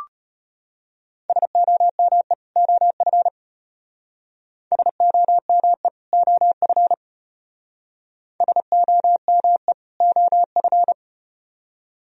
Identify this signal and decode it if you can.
Morse code, heard as a keyed tone, about 33 wpm, 700 Hz. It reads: SOME OF SOME OF SOME OF